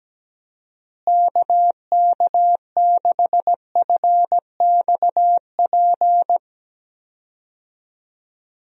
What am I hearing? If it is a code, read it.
Morse code, heard as a keyed tone, about 17 wpm, 700 Hz. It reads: KK6FXP
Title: USA Call Signs - 6 Characters - Repeated 2x - 17wpm